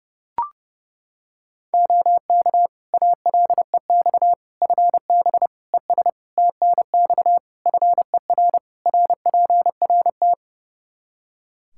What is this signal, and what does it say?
Morse code, heard as a keyed tone, about 30 wpm, 700 Hz. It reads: OK ALEX FB ES TNX FER RPRT
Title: Practice QSO - Exchange 2 - Info - 30wpm